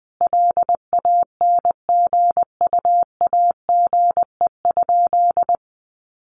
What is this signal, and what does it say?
Morse code, heard as a keyed tone, about 20 wpm, 700 Hz. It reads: LANGUAGE?
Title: Sentences from Top 1100 Words 20wpm